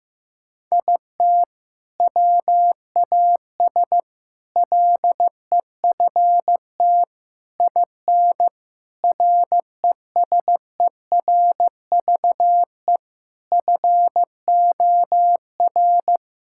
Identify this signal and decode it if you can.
Morse code, heard as a keyed tone, about 15 wpm, 700 Hz. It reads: IT WAS LEFT IN RESERVE FOR
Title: Sentences from Top 1000 Words - Review Part 6 - 15wpm